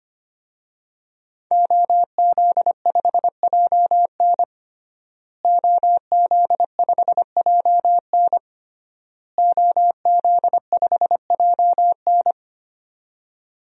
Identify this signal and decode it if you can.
Morse code, heard as a keyed tone, about 25 wpm, 700 Hz. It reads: OZ5JN OZ5JN OZ5JN
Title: DXCC-100 Call Signs - 5 Characters - Repeated 3x - 25wpm